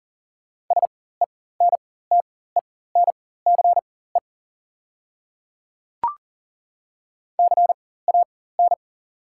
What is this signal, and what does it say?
Morse code, heard as a keyed tone, about 40 wpm, spaced as for 19 wpm, 700 Hz. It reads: SENTENCE CAN
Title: 4x Character Spacing - Top 200 Words - 40wpm